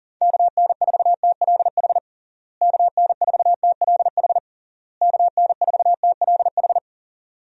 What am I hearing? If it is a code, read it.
Morse code, heard as a keyed tone, about 37 wpm, 700 Hz. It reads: KN4TLH KN4TLH KN4TLH
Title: USA Call Signs - 6 Characters - Repeated 3x - 40wpm